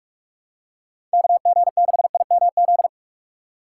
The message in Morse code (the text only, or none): KG6IMZ